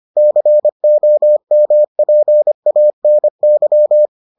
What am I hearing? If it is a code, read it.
Morse code, heard as a keyed tone, about 25 wpm, 600 Hz. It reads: COMPANY